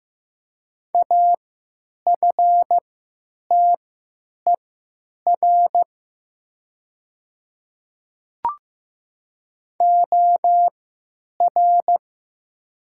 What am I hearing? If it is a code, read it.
Morse code, heard as a keyed tone, about 15 wpm, spaced as for 8 wpm, 700 Hz. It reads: AFTER OR
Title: 3x Character Spacing - Top 200 Words - 15wpm